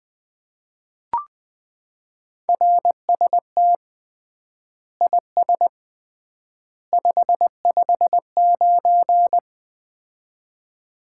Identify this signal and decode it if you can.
Morse code, heard as a keyed tone, about 21 wpm, 700 Hz. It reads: RST IS 559
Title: Common QSO Phrases - 3x Word-Spacing - 20wpm